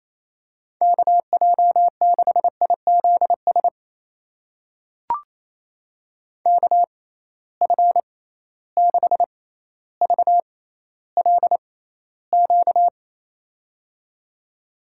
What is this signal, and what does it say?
Morse code, heard as a keyed tone, about 28 wpm, spaced as for 14 wpm, 700 Hz. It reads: KJ6IZS KF6VLQ
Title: USA Call Signs - 6 Characters - 6x Character Spacing - 28wpm